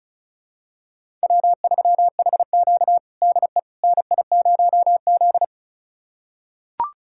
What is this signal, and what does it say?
Morse code, heard as a keyed tone, about 36 wpm, 700 Hz. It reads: W3HQ DE NI0Z
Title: USA Callsign DE Callsign - 4 Characters - 35wpm